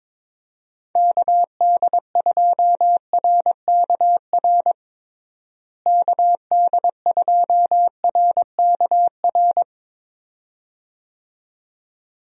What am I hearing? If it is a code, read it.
Morse code, heard as a keyed tone, about 22 wpm, 700 Hz. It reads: KD2RKR KD2RKR
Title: USA Call Signs - 6 Characters - Repeated 2x - 22wpm